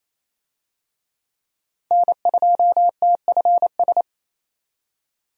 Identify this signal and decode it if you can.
Morse code, heard as a keyed tone, about 28 wpm, 700 Hz. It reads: N2TFS
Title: USA Call Signs - 5 Characters - 28wpm